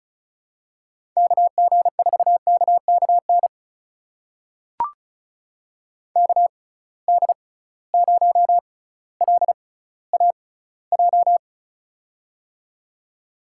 Morse code, heard as a keyed tone, about 34 wpm, spaced as for 18 wpm, 700 Hz. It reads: KG4KKN KD0LAJ
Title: USA Call Signs - 6 Characters - 6x Character Spacing - 35wpm